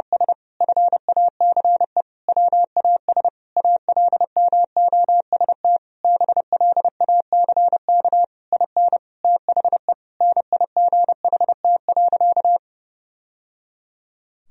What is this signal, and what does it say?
Morse code, heard as a keyed tone, about 30 wpm, 700 Hz. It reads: S FACE WAS ALMOST BLACK IN THE NIGHT.
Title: Sentences from Top 1000 Words - Review Part 4 - 30wpm